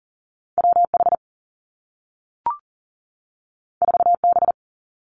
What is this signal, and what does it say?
Morse code, heard as a keyed tone, about 40 wpm, 700 Hz. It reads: WH 4B
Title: Two Challenging Letter-Number Combinations 40wpm